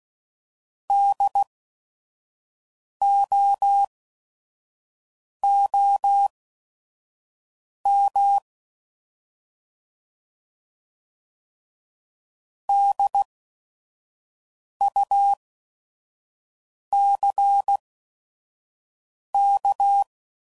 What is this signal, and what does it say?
Morse code, heard as a keyed tone, about 16 wpm, spaced as for 5 wpm, 800 Hz. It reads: DOOM DUCK
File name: Prefixes / By First Letter / Ds